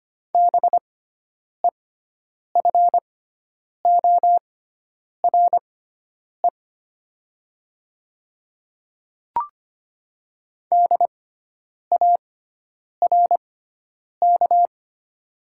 Morse code, heard as a keyed tone, about 25 wpm, spaced as for 9 wpm, 700 Hz. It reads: BEFORE DARK
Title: 6x Character Spacing - Top 500 Words - 25wpm